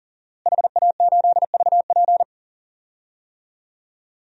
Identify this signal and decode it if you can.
Morse code, heard as a keyed tone, about 40 wpm, 700 Hz. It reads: HA8VP
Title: DXCC-100 Call Signs - 5 Characters - Repeated 3x - 40wpm